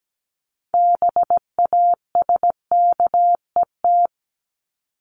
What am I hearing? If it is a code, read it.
Morse code, heard as a keyed tone, about 17 wpm, 700 Hz. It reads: BASKET
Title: Top 1751-2000 Words 17wpm